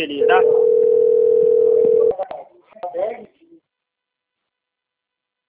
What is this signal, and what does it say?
Ringback tone (North America), 440 + 480 Hz, with speech.
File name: out- - - - - .36657